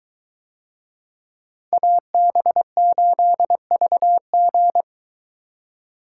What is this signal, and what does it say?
Morse code, heard as a keyed tone, about 23 wpm, 700 Hz. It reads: AB8VG